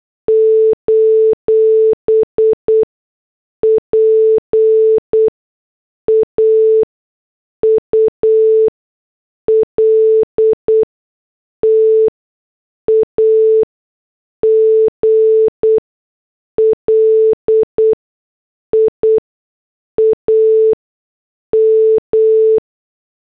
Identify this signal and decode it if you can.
Morse code, heard as a keyed tone, about 8 wpm, spaced as for 6 wpm, 440 Hz. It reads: : PAUL TAGLIAM